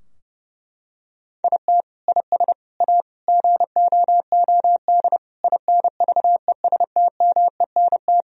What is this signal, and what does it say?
Morse code, heard as a keyed tone, about 30 wpm, 700 Hz. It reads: IT IS A GOOD INVESTMENT
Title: Sentences from Top 1000 Words - Review Part 4 - 30wpm